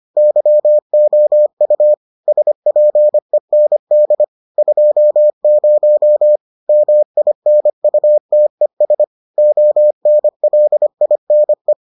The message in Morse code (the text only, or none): YOU SPEND 20 MINUTES ONLINE